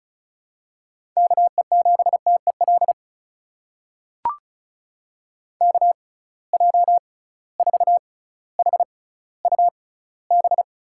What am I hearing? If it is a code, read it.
Morse code, heard as a keyed tone, about 36 wpm, spaced as for 18 wpm, 700 Hz. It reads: KE7TEL KJ4HUB
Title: USA Call Signs - 6 Characters - 6x Character Spacing - 35wpm